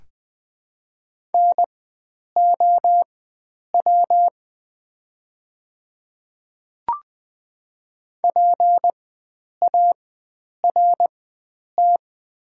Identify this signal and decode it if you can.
Morse code, heard as a keyed tone, about 20 wpm, spaced as for 9 wpm, 690 Hz. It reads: NOW PART
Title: 4x Character Spacing - Top 300 Words - 20wpm